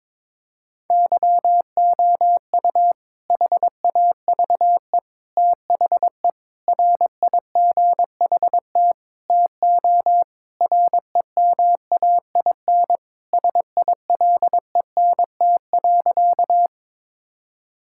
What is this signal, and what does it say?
Morse code, heard as a keyed tone, about 22 wpm, 700 Hz. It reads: YOU HAVE THE RIGHT TO REMAIN SILENT.
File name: practice / 22wpm / Sentences with 4-Word Phrases - 22wpm